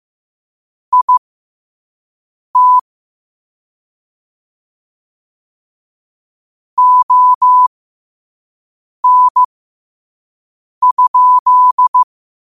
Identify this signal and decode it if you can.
Morse code, heard as a keyed tone, about 15 wpm, spaced as for 5 wpm, 1000 Hz. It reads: IT ON?
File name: qotdfiles / 5wpm